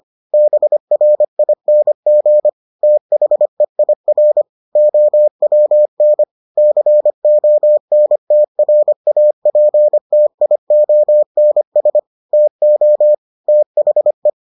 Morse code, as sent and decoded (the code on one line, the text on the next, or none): -... .-. .. -. --. / - .... .. .-. / --- .-- -. / -.-. --- -. - .-. .- .--. - .. --- -. ... / - --- / - .... .
BRING THEIR OWN CONTRAPTIONS TO THE